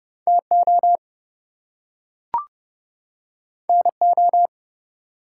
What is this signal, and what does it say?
Morse code, heard as a keyed tone, about 30 wpm, 700 Hz. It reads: TO NO
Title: Two Letter Words - 30wpm